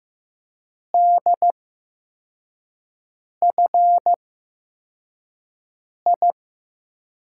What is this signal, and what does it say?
Morse code, heard as a keyed tone, about 15 wpm, 700 Hz. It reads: D F I
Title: USA Call Signs - 5 Characters - 8x Character Spacing - 15wpm